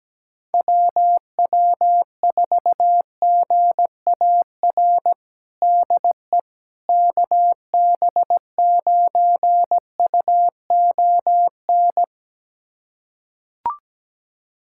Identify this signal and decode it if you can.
Morse code, heard as a keyed tone, about 17 wpm, 700 Hz. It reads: WW4GAR DE KB9UON
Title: USA Callsign DE Callsign - 6 Characters - 17wpm